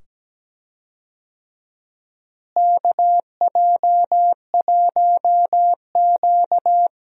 Morse code, heard as a keyed tone, about 17 wpm, 700 Hz. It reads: KJ1Q
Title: USA Call Signs - 4 Characters - Repeated 2x - 17wpm